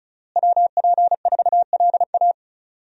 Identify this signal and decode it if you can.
Morse code, heard as a keyed tone, about 36 wpm, 700 Hz. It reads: WP4LA
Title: USA Call Signs - 5 Characters - Repeated 3x - 35wpm